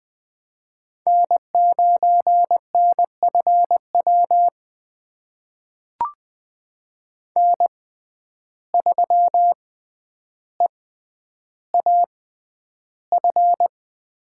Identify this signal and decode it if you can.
Morse code, heard as a keyed tone, about 20 wpm, spaced as for 10 wpm, 700 Hz. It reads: N9NFW N3EAF
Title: USA Call Signs - 5 Characters - 6x Character Spacing - 20wpm